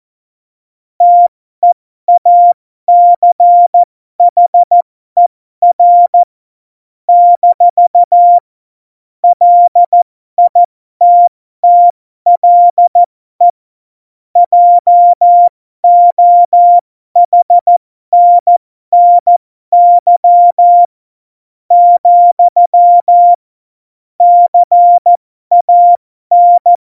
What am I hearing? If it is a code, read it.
Morse code, heard as a keyed tone, about 14 wpm, spaced as for 12 wpm, 700 Hz. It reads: TEACHER - LITTLE JOHNNY , CAN
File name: Morsekurs / Witze